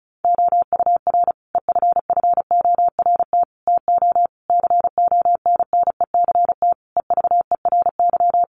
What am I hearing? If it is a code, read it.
Morse code, heard as a keyed tone, about 34 wpm, 700 Hz. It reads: OUR EFFORT TO CONNECT EVERY
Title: Sentences from Top 1500 Words 35wpm